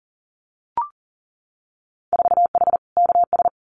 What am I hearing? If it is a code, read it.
Morse code, heard as a keyed tone, about 40 wpm, 700 Hz. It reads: 4H KS
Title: Winter Field Day Exchanges Encore 40wpm